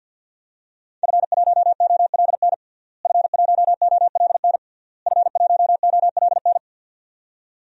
Morse code, heard as a keyed tone, about 50 wpm, 700 Hz. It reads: F1OLN F1OLN F1OLN
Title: DXCC-100 Call Signs - 5 Characters - Repeated 3x - 50wpm